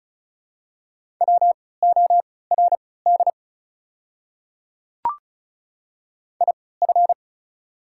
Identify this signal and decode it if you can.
Morse code, heard as a keyed tone, about 34 wpm, spaced as for 20 wpm, 690 Hz. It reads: WORD IF
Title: 3x Character Spacing - Top 100 Words - 35wpm